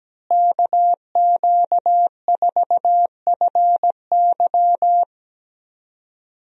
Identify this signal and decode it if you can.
Morse code, heard as a keyed tone, about 17 wpm, 700 Hz. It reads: KQ4FY
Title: USA Call Signs - 5 Characters - Repeated 2x - 17wpm